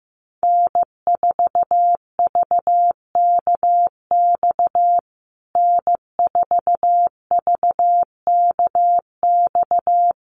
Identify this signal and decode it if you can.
Morse code, heard as a keyed tone, about 15 wpm, 700 Hz. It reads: N4VKX N4VKX
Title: USA Call Signs Repeated 3x 15wpm